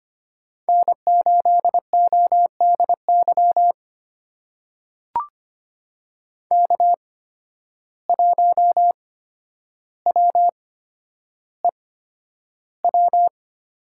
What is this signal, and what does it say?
Morse code, heard as a keyed tone, about 25 wpm, spaced as for 11 wpm, 700 Hz. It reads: N8ODY K1WEW
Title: USA Call Signs - 5 Characters - 8x Character Spacing - 25wpm